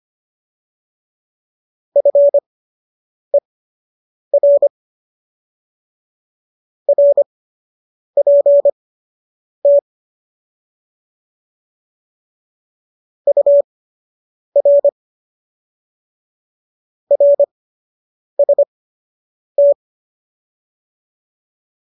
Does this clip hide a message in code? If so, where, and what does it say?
Yes: Morse code from 2.0 to 19.7 s, heard as a keyed tone, about 25 wpm, spaced as for 8 wpm, 570 Hz. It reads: FER RPT UR RST